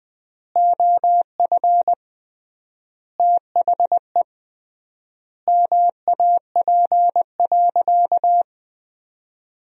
Morse code, heard as a keyed tone, about 20 wpm, 700 Hz. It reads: OF THE MAP.
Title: 3x Word-Spacing - Sentences from Top 400 Words - 20wpm